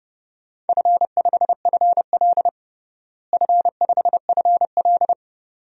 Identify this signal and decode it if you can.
Morse code, heard as a keyed tone, about 30 wpm, 700 Hz. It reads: F5FL F5FL
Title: DXCC-100 Call Signs - 4 Characters - Repeated 2x - 30wpm